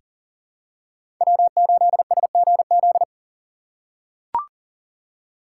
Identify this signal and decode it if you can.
Morse code, heard as a keyed tone, about 40 wpm, 700 Hz. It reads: W8SGZ
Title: USA Call Signs - 5 Characters - 4x Character Spacing - 40wpm